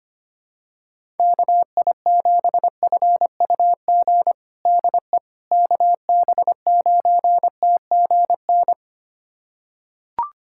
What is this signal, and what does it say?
Morse code, heard as a keyed tone, about 25 wpm, 700 Hz. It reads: KI7FUG DE KB9TGN
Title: USA Callsign DE Callsign - 6 Characters - 25wpm